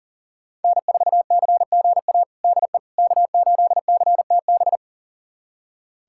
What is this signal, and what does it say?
Morse code, heard as a keyed tone, about 40 wpm, 700 Hz. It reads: N4CGA DE K8CTB
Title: USA Callsign DE Callsign - 5 Characters - 40wpm